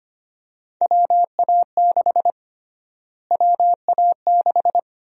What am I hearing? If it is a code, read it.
Morse code, heard as a keyed tone, about 25 wpm, 700 Hz. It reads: WA6 WA6
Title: USA Call Sign Prefixes - Repeated 2x - 25wpm